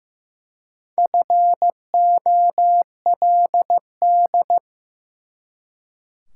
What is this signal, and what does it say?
Morse code, heard as a keyed tone, about 15 wpm, 700 Hz. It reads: FOLD
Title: Four Letter Words - 15wpm